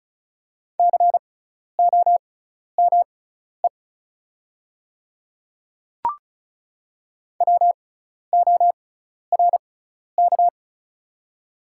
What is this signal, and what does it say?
Morse code, heard as a keyed tone, about 36 wpm, spaced as for 12 wpm, 700 Hz. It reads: COME WORK
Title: 6x Character Spacing - Top 300 Words - 35wpm